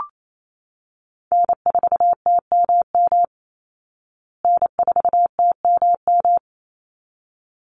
Morse code, heard as a keyed tone, about 28 wpm, 700 Hz. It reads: N4TMM N4TMM
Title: Call Signs Repeated 3x - 28wpm